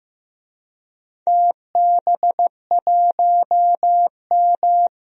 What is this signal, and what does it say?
Morse code, heard as a keyed tone, about 15 wpm, 700 Hz. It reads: TB1M
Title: DXCC-100 Call Signs - 4 Characters - 15wpm